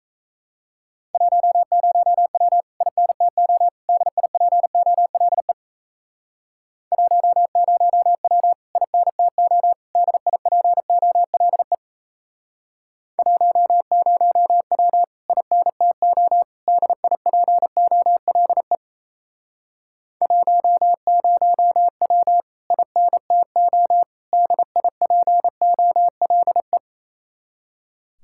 Morse code, heard as a keyed tone, about 35 wpm, 700 Hz. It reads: 10W INTO DIPOLE 10W INTO DIPOLE 10W INTO DIPOLE 10W INTO DIPOLE